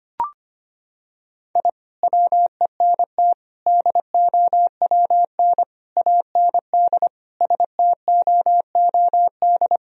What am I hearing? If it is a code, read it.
Morse code, heard as a keyed tone, about 25 wpm, 700 Hz. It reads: I WENT DOWN AND STOOD
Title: Sentences from Top 1000 Words - Review Part 2 - 25wpm